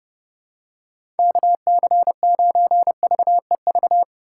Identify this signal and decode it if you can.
Morse code, heard as a keyed tone, about 32 wpm, 700 Hz. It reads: KC9VEV
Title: USA Call Signs - 6 Characters - 3x Character Spacing - 30wpm